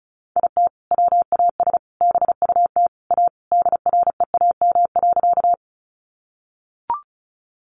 Morse code, heard as a keyed tone, about 34 wpm, 700 Hz. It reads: IT WAS BUT A DREAM.
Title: Sentences from Top 1500 Words 35wpm